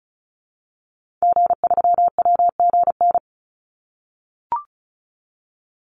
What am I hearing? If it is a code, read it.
Morse code, heard as a keyed tone, about 34 wpm, 700 Hz. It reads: G3WGN
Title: Call Signs - 35wpm